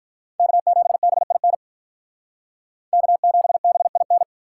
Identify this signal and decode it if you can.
Morse code, heard as a keyed tone, about 45 wpm, 700 Hz. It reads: K7BIN K7BIN